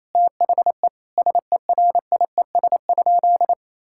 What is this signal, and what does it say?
Morse code, heard as a keyed tone, about 27 wpm, 700 Hz. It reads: THE SERIES?
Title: Sentences from Top 1000 Words - Review Part 5 - 28wpm